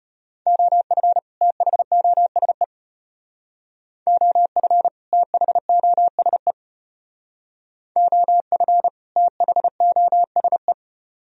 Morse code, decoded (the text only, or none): OF THOSE OF THOSE OF THOSE